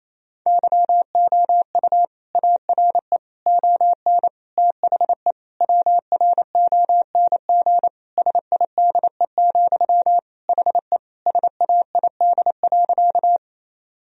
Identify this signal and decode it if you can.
Morse code, heard as a keyed tone, about 28 wpm, 700 Hz. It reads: YOU ARE ON THE WRONG SIDE, HE SAID.
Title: Sentences from Top 1000 Words - Review Part 1 - 28wpm